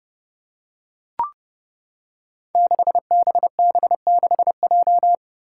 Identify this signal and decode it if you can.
Morse code, heard as a keyed tone, about 30 wpm, 700 Hz. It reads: 6BB6J
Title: Five Challenging Letter-Number Combinations 30wpm